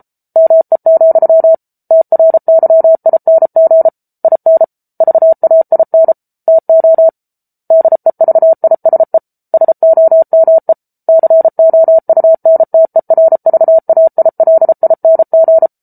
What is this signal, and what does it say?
Morse code, heard as a keyed tone, about 34 wpm, 650 Hz. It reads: ME, TRYING IN VAIN TO DEVISE SOME COUNTERVAILING